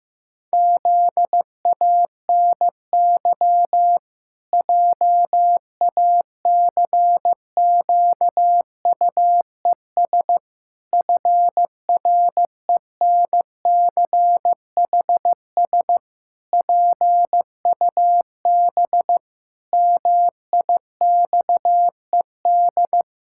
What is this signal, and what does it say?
Morse code, heard as a keyed tone, about 15 wpm, 700 Hz. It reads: ZANY JACQUES FRENCHS PUB MIXED
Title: Pangram Challenge v2 - 15wpm